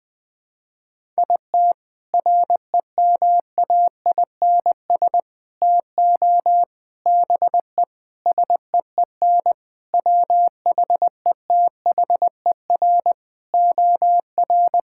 Morse code, heard as a keyed tone, about 20 wpm, 700 Hz. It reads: IT REMAINS TO BE SEEN WHETHER OR
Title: Sentences with 8-Word Phrases - 20wpm